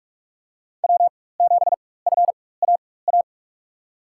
Morse code, heard as a keyed tone, about 46 wpm, spaced as for 21 wpm, 700 Hz. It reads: W7FAA
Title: USA Call Signs - 5 Characters - 4x Character Spacing - 45wpm